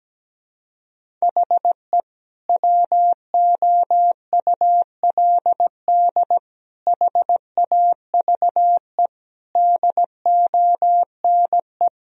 Morse code, heard as a keyed tone, about 17 wpm, 700 Hz. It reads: HE WOULD HAVE DONE
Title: Sentences from Top 1000 Words - Review Part 1 - 17wpm